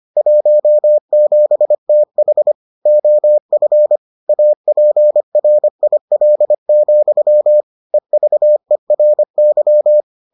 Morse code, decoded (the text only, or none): 17TH OF APRIL, EVERY